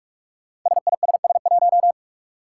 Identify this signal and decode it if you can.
Morse code, heard as a keyed tone, about 46 wpm, 700 Hz. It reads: SISS1
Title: Five Challenging Letter-Number Combinations 45wpm